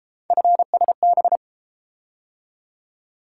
Morse code, heard as a keyed tone, about 31 wpm, 700 Hz. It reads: FSB